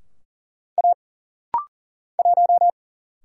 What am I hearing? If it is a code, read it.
Morse code, heard as a keyed tone, about 40 wpm, 700 Hz. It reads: A 1
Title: Single Letter-Number - Rapid-Fire 40wpm